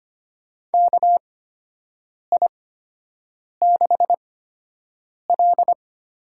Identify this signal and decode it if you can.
Morse code, heard as a keyed tone, about 25 wpm, spaced as for 7 wpm, 700 Hz. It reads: KI6L
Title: USA Call Signs - 8x Character Spacing - 25wpm